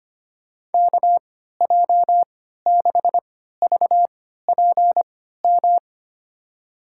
Morse code, heard as a keyed tone, about 25 wpm, spaced as for 14 wpm, 700 Hz. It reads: KJ6VPM
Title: USA Call Signs - 6 Characters - 3x Character Spacing - 25wpm